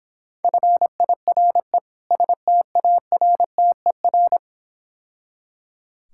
Morse code, heard as a keyed tone, about 26 wpm, 700 Hz. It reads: FIRE STARTER